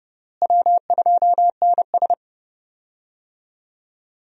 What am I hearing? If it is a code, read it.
Morse code, heard as a keyed tone, about 32 wpm, 700 Hz. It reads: W2NS
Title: USA Call Signs - 4 Characters - Repeated 2x - 30wpm